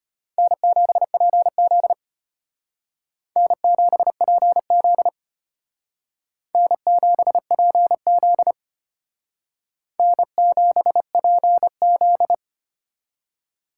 Morse code, heard as a keyed tone, about 32 wpm, 700 Hz. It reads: N7PZ N7PZ N7PZ N7PZ